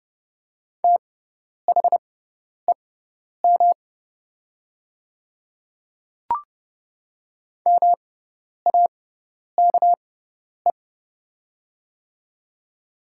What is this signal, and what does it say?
Morse code, heard as a keyed tone, about 31 wpm, spaced as for 10 wpm, 700 Hz. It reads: THEM MAKE
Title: 6x Character Spacing - Top 100 Words - 30wpm